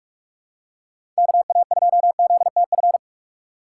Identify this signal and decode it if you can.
Morse code, heard as a keyed tone, about 43 wpm, 700 Hz. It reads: KA2ZTF
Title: USA Call Signs - 6 Characters - 2x Character Spacing - 45wpm